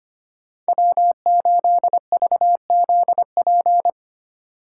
Morse code, heard as a keyed tone, about 29 wpm, 700 Hz. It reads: W8VZP
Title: Speed Racing - USA 1x3 Call Signs 25wpm